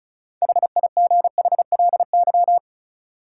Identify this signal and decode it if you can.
Morse code, heard as a keyed tone, about 36 wpm, 700 Hz. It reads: HIGHLY
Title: Six Letter Words - 35wpm